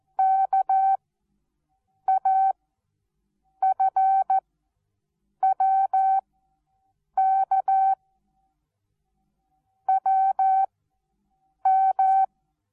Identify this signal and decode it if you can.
Morse code, heard as a keyed tone, about 14 wpm, 770 Hz. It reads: K A F W K W M
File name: Onlinelehrgang / morsen / DJ4UF Morsen19